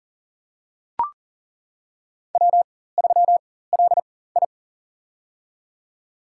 Morse code, heard as a keyed tone, about 40 wpm, spaced as for 19 wpm, 700 Hz. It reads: W3LI
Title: USA Call Signs - 4 Characters - 4x Character Spacing - 40wpm